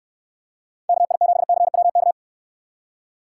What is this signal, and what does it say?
Morse code, heard as a keyed tone, about 66 wpm, 700 Hz. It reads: KI7CWK